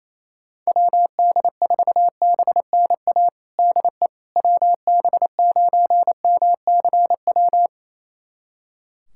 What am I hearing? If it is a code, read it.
Morse code, heard as a keyed tone, about 28 wpm, 700 Hz. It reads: WD4BNA DE WB9MCW
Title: USA Callsign DE Callsign - 6 Characters - 28wpm